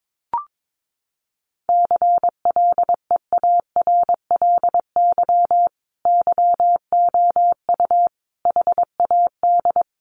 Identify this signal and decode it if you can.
Morse code, heard as a keyed tone, about 22 wpm, 700 Hz. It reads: CLEARLY YOU HAD
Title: Sentences from Top 1300 Words 22wpm